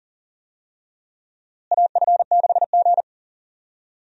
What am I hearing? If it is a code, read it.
Morse code, heard as a keyed tone, about 40 wpm, 700 Hz. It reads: AF6G
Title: USA Call Signs - 4 Characters - Repeated 2x - 40wpm